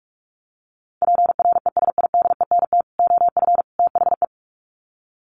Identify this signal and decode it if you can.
Morse code, heard as a keyed tone, about 44 wpm, 700 Hz. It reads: PRESIDENT OF THE
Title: Sets of 3 Words 45wpm